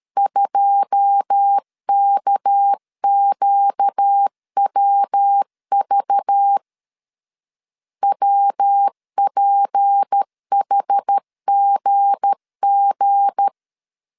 Morse code, heard as a keyed tone, about 13 wpm, 800 Hz. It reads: 2KQWV WPHGG
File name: CW / DET / 1 9 1500C S C 12wpm 3811